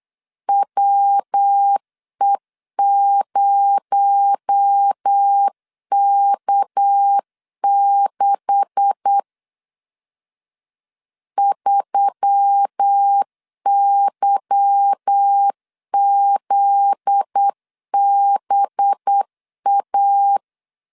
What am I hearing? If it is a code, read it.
Morse code, heard as a keyed tone, about 8 wpm, 800 Hz. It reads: WE0K6 3YZBA
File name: CW / DET / 1 9 1500C S C 8wpm 5653